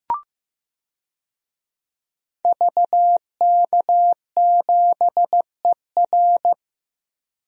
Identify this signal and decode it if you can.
Morse code, heard as a keyed tone, about 15 wpm, 700 Hz. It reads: VK7ER
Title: DXCC-100 Call Signs - 5 Characters - Repeated 2x - 15wpm